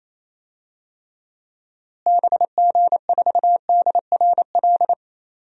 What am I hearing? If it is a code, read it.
Morse code, heard as a keyed tone, about 28 wpm, 700 Hz. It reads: BG4DRL
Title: DXCC-100 Call Signs - 6 Characters - Repeated 3x - 28wpm